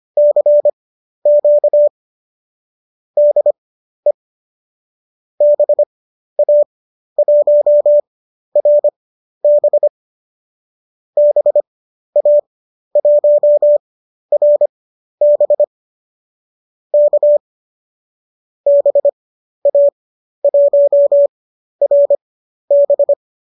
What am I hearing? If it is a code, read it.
Morse code, heard as a keyed tone, about 25 wpm, spaced as for 12 wpm, 590 Hz. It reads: CQ DE BA1RB BA1RB K BA1RB
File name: qso002 12